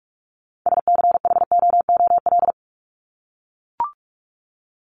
Morse code, heard as a keyed tone, about 44 wpm, 700 Hz. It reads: SCHOOL
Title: Top 300 Words 45wpm